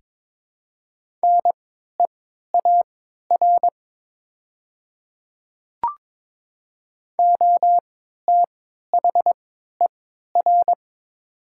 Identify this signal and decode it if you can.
Morse code, heard as a keyed tone, about 22 wpm, spaced as for 12 wpm, 700 Hz. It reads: NEAR OTHER
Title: 3x Character Spacing - Top 300 Words - 22wpm